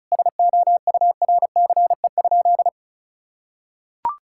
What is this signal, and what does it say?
Morse code, heard as a keyed tone, about 36 wpm, 700 Hz. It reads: SOURCE?
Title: Sentences from Top 1000 Words - Review Part 5 - 35wpm